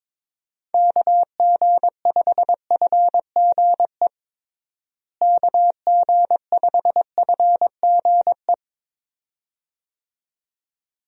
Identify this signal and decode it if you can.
Morse code, heard as a keyed tone, about 22 wpm, 700 Hz. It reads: KG5FGE KG5FGE
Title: USA Call Signs - 6 Characters - Repeated 2x - 22wpm